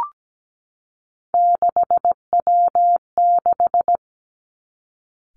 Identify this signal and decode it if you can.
Morse code, heard as a keyed tone, about 17 wpm, 700 Hz. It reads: 6W6
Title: Three Challenging Letter-Number Combinations 17wpm